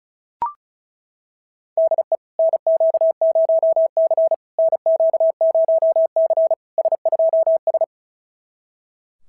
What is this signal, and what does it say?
Morse code, heard as a keyed tone, about 33 wpm, 650 Hz. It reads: DE NQ0C NQ0C S2S
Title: SOTA QSO Practice 35wpm